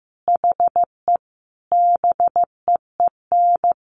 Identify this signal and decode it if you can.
Morse code, heard as a keyed tone, about 15 wpm, 700 Hz. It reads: HE BEEN
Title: Sentences from Top 100 Words 15wpm